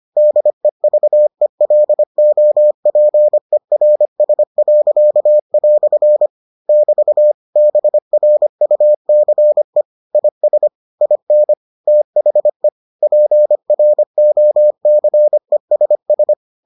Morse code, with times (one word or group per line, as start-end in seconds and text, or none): 0.2-6.3 DEVELOPERS."
6.7-7.3 <BT>
7.6-9.8 BRUCE
10.2-10.7 IS
11.0-11.5 IN
11.9-12.7 THE
13.0-16.3 PROCESS